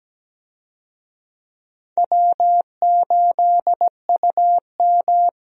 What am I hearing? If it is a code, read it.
Morse code, heard as a keyed tone, about 17 wpm, 700 Hz. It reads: W8UM
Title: USA Call Signs - 4 Characters - 17wpm